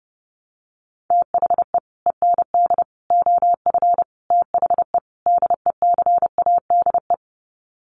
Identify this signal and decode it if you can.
Morse code, heard as a keyed tone, about 30 wpm, 700 Hz. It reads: THE END OF THE DECADE
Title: Sets of 5 Words Encore 30wpm